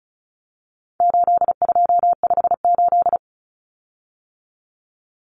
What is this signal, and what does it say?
Morse code, heard as a keyed tone, about 34 wpm, 700 Hz. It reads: 8258
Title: Common Cut Numbers - 4-Digit 35wpm